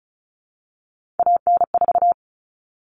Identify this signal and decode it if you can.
Morse code, heard as a keyed tone, about 34 wpm, 700 Hz. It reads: AN4
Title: All Cut Numbers - 3-Digit 35wpm